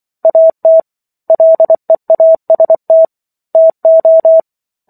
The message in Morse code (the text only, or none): AT LEAST TO